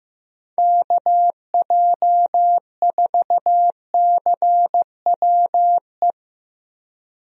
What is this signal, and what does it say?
Morse code, heard as a keyed tone, about 15 wpm, 700 Hz. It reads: KJ4CWE
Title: USA Call Signs - 6 Characters - Repeated 2x - 15wpm